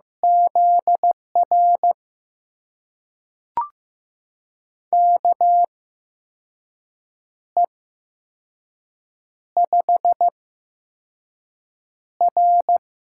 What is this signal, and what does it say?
Morse code, heard as a keyed tone, about 15 wpm, spaced as for 5 wpm, 700 Hz. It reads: ZR KE5R
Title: USA Call Signs - 6 Characters - 8x Character Spacing - 15wpm